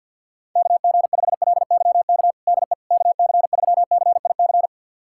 Morse code, heard as a keyed tone, about 52 wpm, 700 Hz. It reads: KG5LYK DE KC3CIC
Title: USA Callsign DE Callsign - 6 Characters - 50wpm